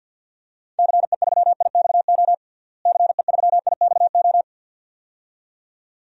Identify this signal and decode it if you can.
Morse code, heard as a keyed tone, about 47 wpm, 700 Hz. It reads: CE3IXO CE3IXO
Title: DXCC-100 Call Signs - 6 Characters - Repeated 2x - 50wpm